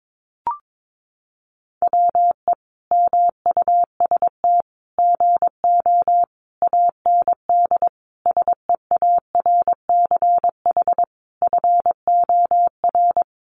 Morse code, heard as a keyed tone, about 22 wpm, 700 Hz. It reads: WE MUST GO AND SEARCH FOR
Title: Sentences from Top 800 Words 22wpm